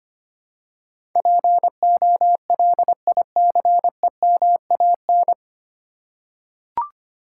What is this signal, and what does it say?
Morse code, heard as a keyed tone, about 25 wpm, 700 Hz. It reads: POLICEMAN
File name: Nine Letter Words - 25wpm